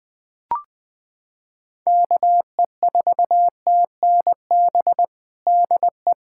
Morse code, heard as a keyed tone, about 20 wpm, 700 Hz. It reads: KE4TNB DE
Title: Practice QSO - Exchange 4 - Ending - 20wpm